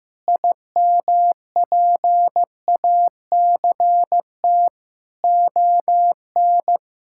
Morse code, heard as a keyed tone, about 15 wpm, 700 Hz. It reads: IMPACT ON
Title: Sentences from Top 1000 Words - Review Part 6 - 15wpm